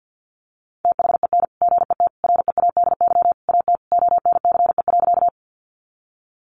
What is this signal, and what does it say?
Morse code, heard as a keyed tone, about 50 wpm, 700 Hz. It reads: THEN GET READY AT ONCE.
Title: Sentences from Top 400 Words 50wpm